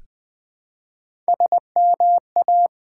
Morse code, heard as a keyed tone, about 20 wpm, 700 Hz. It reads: SMA
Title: Build A Word - 1 Letter At A Time - 5-Letter Words - 20wpm